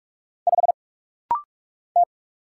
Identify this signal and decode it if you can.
Morse code, heard as a keyed tone, about 44 wpm, 700 Hz. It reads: 5 T
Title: Single Letter-Number Alt - Rapid-Fire 45wpm